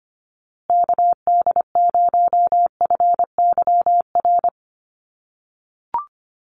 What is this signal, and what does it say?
Morse code, heard as a keyed tone, about 25 wpm, 700 Hz. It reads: KD0FYR
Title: USA Call Signs Repeated 2x 25wpm